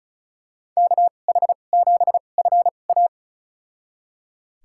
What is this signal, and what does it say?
Morse code, heard as a keyed tone, about 34 wpm, spaced as for 25 wpm, 700 Hz. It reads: KH7FA
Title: USA Call Signs - 5 Characters - 2x Character Spacing - 35wpm